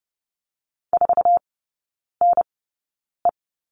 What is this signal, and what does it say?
Morse code, heard as a keyed tone, about 30 wpm, spaced as for 9 wpm, 700 Hz. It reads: 4NE
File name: course / 30:10wpm / 087 Lesson 15e - Number 5 - Three Characters 30wpm